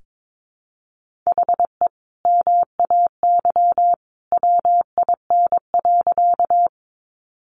Morse code, heard as a keyed tone, about 22 wpm, 700 Hz. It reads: HE MAY WIN.
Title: Sentences from Top 1500 Words 22wpm